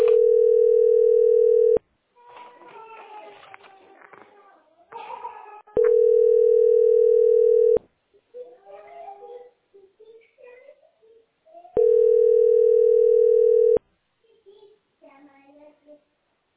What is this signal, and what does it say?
Ringback tone (North America), 440 + 480 Hz.